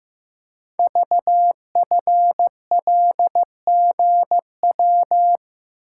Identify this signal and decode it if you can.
Morse code, heard as a keyed tone, about 15 wpm, 700 Hz. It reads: VFLGW
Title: Code Groups - 5 Letters - 15wpm